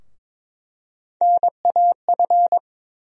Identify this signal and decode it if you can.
Morse code, heard as a keyed tone, about 22 wpm, 700 Hz. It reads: NAF
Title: USA Call Sign Postfixes - 3 Letters 22wpm